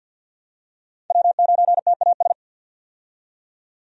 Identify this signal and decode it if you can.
Morse code, heard as a keyed tone, about 50 wpm, 700 Hz. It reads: W9TAS
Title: USA Call Signs - 5 Characters - 50wpm